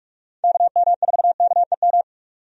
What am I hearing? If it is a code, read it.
Morse code, heard as a keyed tone, about 45 wpm, 700 Hz. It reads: KM4KEM
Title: USA Call Signs - 6 Characters - 45wpm